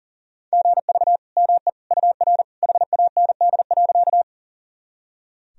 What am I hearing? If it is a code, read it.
Morse code, heard as a keyed tone, about 37 wpm, 700 Hz. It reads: GV ME UR HAND.
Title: Sentences from Top 200 Words and Abbreviations 40wpm